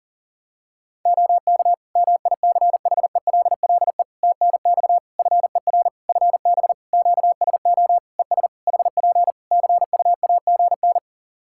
Morse code, heard as a keyed tone, about 37 wpm, 700 Hz. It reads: OK MICHELLE TNX FER FB QSO ES HP CUAGN
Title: Practice QSO - Exchange 4 - Ending - 40wpm